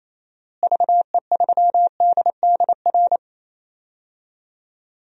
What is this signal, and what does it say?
Morse code, heard as a keyed tone, about 28 wpm, 700 Hz. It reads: VE3DDR
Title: DXCC-100 Call Signs - 6 Characters - Repeated 3x - 28wpm